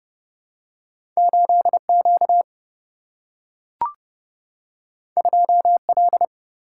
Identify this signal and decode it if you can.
Morse code, heard as a keyed tone, about 30 wpm, 700 Hz. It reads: 8Q 2L
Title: Single-Letter Single-Number 30wpm